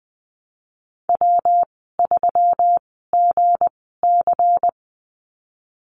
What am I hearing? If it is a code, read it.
Morse code, heard as a keyed tone, about 19 wpm, spaced as for 14 wpm, 700 Hz. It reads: W3GC
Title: 173 Lesson 28g - Number 9 - Call Signs 20wpm